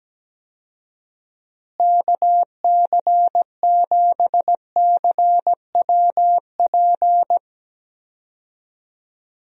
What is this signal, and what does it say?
Morse code, heard as a keyed tone, about 17 wpm, 700 Hz. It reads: KC7CWP
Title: USA Call Signs - 6 Characters - Repeated 3x - 17wpm